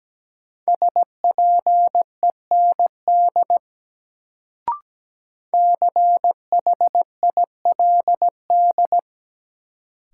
Morse code, heard as a keyed tone, about 17 wpm, 700 Hz. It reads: SPEND CHILD
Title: Sentences from Top 800 Words - New Word Drill - 17wpm